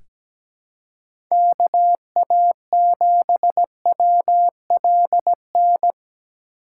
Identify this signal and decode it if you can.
Morse code, heard as a keyed tone, about 17 wpm, 700 Hz. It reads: KA7WLN
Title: USA Call Signs - 6 Characters - 2x Character Spacing - 17wpm